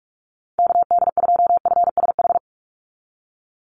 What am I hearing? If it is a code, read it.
Morse code, heard as a keyed tone, about 44 wpm, 700 Hz. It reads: KD2FSH
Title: Call Signs Repeated 3x - 45wpm